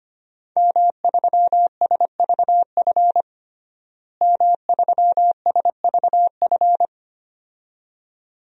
Morse code, heard as a keyed tone, about 25 wpm, 700 Hz. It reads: M3SVF M3SVF
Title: DXCC-100 Call Signs - 5 Characters - Repeated 2x - 25wpm